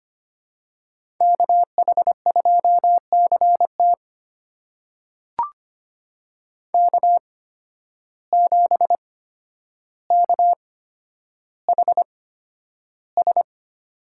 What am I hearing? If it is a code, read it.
Morse code, heard as a keyed tone, about 25 wpm, spaced as for 11 wpm, 700 Hz. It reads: KH2CT K7KHS
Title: USA Call Signs - 5 Characters - 8x Character Spacing - 25wpm